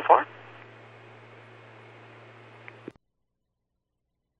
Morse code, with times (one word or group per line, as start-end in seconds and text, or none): none